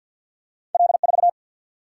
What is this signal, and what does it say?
Morse code, heard as a keyed tone, about 47 wpm, 700 Hz. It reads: L4
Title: Single-Letter Single-Number 50wpm